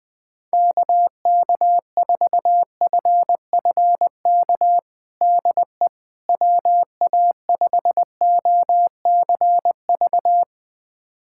Morse code, heard as a keyed tone, about 20 wpm, 700 Hz. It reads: KK4FFK DE WA5OCV
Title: USA Callsign DE Callsign - 6 Characters - 20wpm